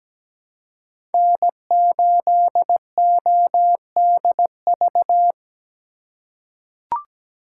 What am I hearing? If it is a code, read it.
Morse code, heard as a keyed tone, about 17 wpm, 700 Hz. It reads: N8ODV
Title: USA Call Signs - 4x Character Spacing - 17wpm